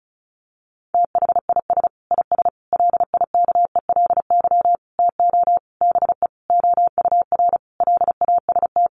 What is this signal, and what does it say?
Morse code, heard as a keyed tone, about 34 wpm, 700 Hz. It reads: THIS IS LIKELY TO BE OUR LAST
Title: Sentences from Top 1400 Words 35wpm